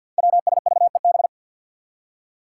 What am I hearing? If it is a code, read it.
Morse code, heard as a keyed tone, about 47 wpm, 700 Hz. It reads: WSVEB
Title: Five Challenging Letter-Number Combinations 50wpm